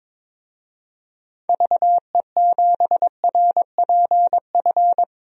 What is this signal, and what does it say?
Morse code, heard as a keyed tone, about 22 wpm, 700 Hz. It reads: VE7RPF
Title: DXCC-100 Call Signs - 6 Characters - Repeated 2x - 22wpm